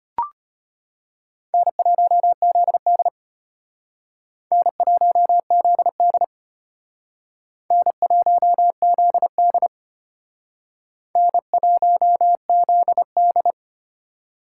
Morse code, heard as a keyed tone, about 32 wpm, 700 Hz. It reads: N1ZD N1ZD N1ZD N1ZD